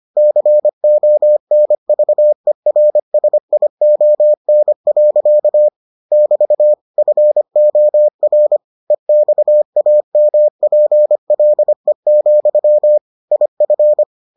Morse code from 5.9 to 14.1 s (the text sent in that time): <BT> FOR EXAMPLE, IF